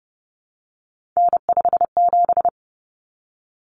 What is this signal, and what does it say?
Morse code, heard as a keyed tone, about 30 wpm, 700 Hz. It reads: N57
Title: Common Cut Numbers - 3-Digit 30wpm